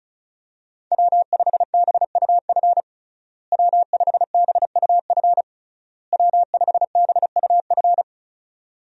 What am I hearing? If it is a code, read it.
Morse code, heard as a keyed tone, about 34 wpm, 700 Hz. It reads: W5BUF W5BUF W5BUF
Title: USA Call Signs - 5 Characters - Repeated 3x - 35wpm